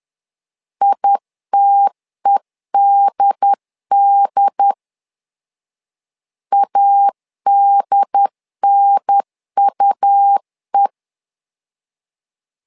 Morse code, heard as a keyed tone, about 10 wpm, 800 Hz. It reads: ITEDD ADNUE
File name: LEZIONI / 2 UHDN / 2 UHDN 10wpm 1904